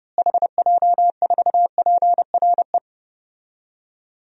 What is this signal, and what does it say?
Morse code, heard as a keyed tone, about 32 wpm, 700 Hz. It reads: HJ4PRE
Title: DXCC-100 Call Signs - 6 Characters - Repeated 2x - 30wpm